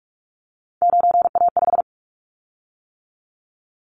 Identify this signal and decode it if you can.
Morse code, heard as a keyed tone, about 46 wpm, 700 Hz. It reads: 9A5
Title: Callsign Prefixes for Top 25 DXCC - 45wpm